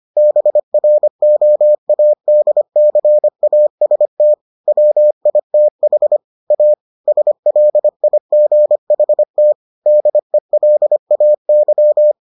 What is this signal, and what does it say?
Morse code, heard as a keyed tone, about 25 wpm, 600 Hz. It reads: BROADCAST WITH A SLIGHT DELAY